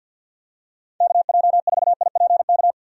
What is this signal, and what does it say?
Morse code, heard as a keyed tone, about 51 wpm, 700 Hz. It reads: KJ4IPK
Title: USA Call Signs - 6 Characters - 50wpm